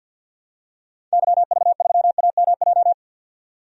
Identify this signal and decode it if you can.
Morse code, heard as a keyed tone, about 51 wpm, 700 Hz. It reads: YV3AMJ
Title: DXCC-100 Call Signs - 6 Characters - 50wpm